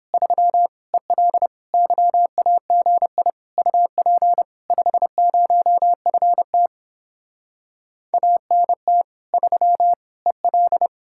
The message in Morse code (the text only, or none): 3 EL YAGI UP 50FT ANT 3 EL